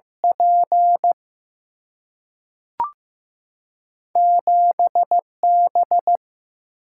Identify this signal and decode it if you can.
Morse code, heard as a keyed tone, about 15 wpm, 700 Hz. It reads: P 7 B
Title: Single-Letter Single-Number 15wpm